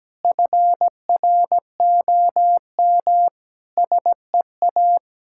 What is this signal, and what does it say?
Morse code, heard as a keyed tone, about 17 wpm, 700 Hz. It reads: FROM SEA
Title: Sentences from Top 1000 Words - Review Part 5 - 17wpm